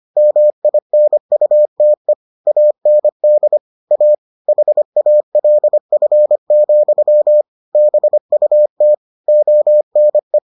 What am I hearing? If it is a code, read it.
Morse code, heard as a keyed tone, about 25 wpm, 600 Hz. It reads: MINUTE AND A HALF, BUT ONE